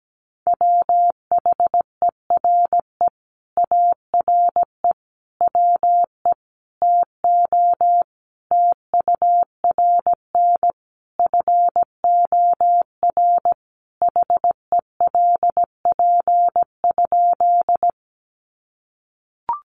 Morse code, heard as a keyed tone, about 17 wpm, 700 Hz. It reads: WHERE ARE WE TO TURN FOR HELP?
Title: Sentences from Top 200 Words 17wpm